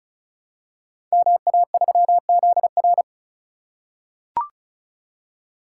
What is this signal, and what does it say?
Morse code, heard as a keyed tone, about 36 wpm, 700 Hz. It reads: MA3ZR
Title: Code Groups - 5 Letter:Numbers - 35wpm